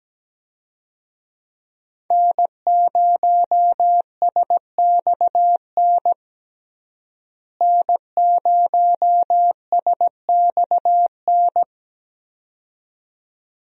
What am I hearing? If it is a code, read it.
Morse code, heard as a keyed tone, about 17 wpm, 700 Hz. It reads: N0SXN N0SXN
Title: USA Call Signs - 5 Characters - Repeated 2x - 17wpm